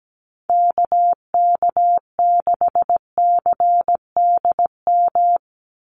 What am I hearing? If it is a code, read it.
Morse code, heard as a keyed tone, about 17 wpm, 700 Hz. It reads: KK6CDM
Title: USA Call Signs 17wpm